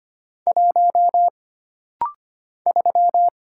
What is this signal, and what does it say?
Morse code, heard as a keyed tone, about 25 wpm, 700 Hz. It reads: 1 3
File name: Single Character (Letter:Number:Punctuation:Prosign) - Rapid-Fire - 25wpm